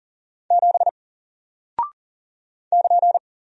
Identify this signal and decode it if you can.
Morse code, heard as a keyed tone, about 40 wpm, 700 Hz. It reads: 7 <KN>
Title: Single Character (Letter:Number:Punctuation:Prosign) Alt 40wpm